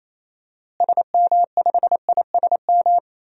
Morse code, heard as a keyed tone, about 28 wpm, 700 Hz. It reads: SM5ISM
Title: DXCC-100 Call Signs - 6 Characters - Repeated 3x - 28wpm